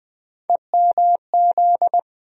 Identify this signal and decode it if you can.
Morse code, heard as a keyed tone, about 20 wpm, 700 Hz. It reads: EMZ
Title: USA Call Sign Postfixes - 3 Letters 20wpm